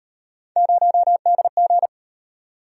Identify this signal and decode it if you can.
Morse code, heard as a keyed tone, about 37 wpm, 700 Hz. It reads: 0DG